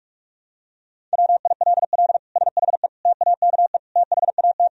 Morse code, heard as a keyed tone, about 45 wpm, 700 Hz. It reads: WILL SHE TAKE THAT
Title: Sentences from Top 1000 Words - Review Part 4 - 45wpm